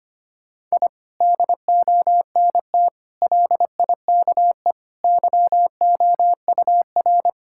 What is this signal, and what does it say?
Morse code, heard as a keyed tone, about 25 wpm, 700 Hz. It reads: I DONT LIKE YOUR
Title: Sentences from Top 1000 Words - Review Part 6 - 25wpm